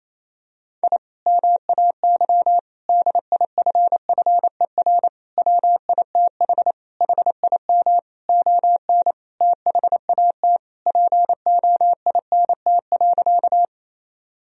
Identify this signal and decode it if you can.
Morse code, heard as a keyed tone, about 28 wpm, 700 Hz. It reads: I MAY DIFFER WITH HIM ON THAT POINT.
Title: Sentences from Top 1000 Words - Review Part 1 - 28wpm